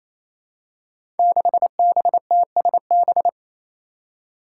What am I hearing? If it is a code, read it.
Morse code, heard as a keyed tone, about 29 wpm, 700 Hz. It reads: 6BTSB
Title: Five Challenging Letter-Number Combinations 28wpm